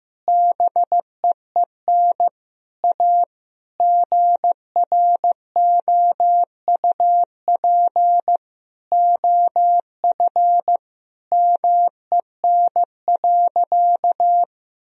Morse code, heard as a keyed tone, about 15 wpm, 700 Hz. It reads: BEEN A GROUP OF MEN.
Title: Sentences from Top 1000 Words - Review Part 1 - 15wpm